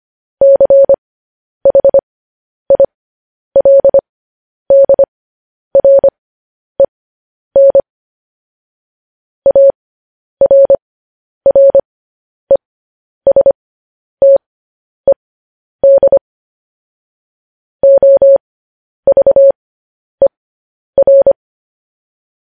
Morse code, heard as a keyed tone, about 25 wpm, spaced as for 10 wpm, 550 Hz. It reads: CHILDREN ARRESTED OVER
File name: podcast / News headlines-1x-25x10